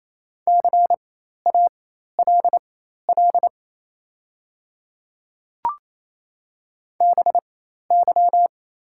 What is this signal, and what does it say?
Morse code, heard as a keyed tone, about 28 wpm, spaced as for 13 wpm, 700 Hz. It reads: CALL BY
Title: 4x Character Spacing - Top 100 Words - 28wpm